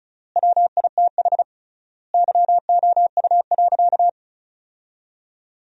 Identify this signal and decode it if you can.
Morse code, heard as a keyed tone, about 36 wpm, 700 Hz. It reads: WITH YOU.
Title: 3x Word-Spacing - Sentences from Top 300 Words - 35wpm